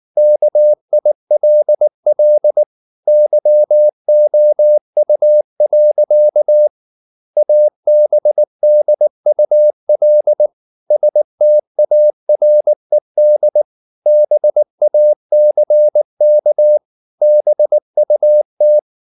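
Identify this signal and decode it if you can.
Morse code, heard as a keyed tone, about 19 wpm, 600 Hz. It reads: KILL YOU. ABDUL STARED BACK BUT
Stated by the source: Morse code at 19 wpm from CW med Gnister, reading Ken Follett's novel Never in English.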